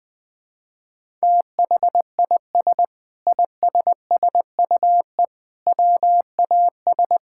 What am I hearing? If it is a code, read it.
Morse code, heard as a keyed tone, about 21 wpm, 700 Hz. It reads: THIS ISSUE WAS
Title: Sentences with 7-Word Phrases - 20wpm